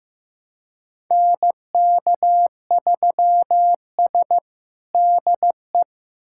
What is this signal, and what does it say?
Morse code, heard as a keyed tone, about 15 wpm, 700 Hz. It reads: NK3S DE
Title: USA Callsign DE Callsign - 4 Characters - 15wpm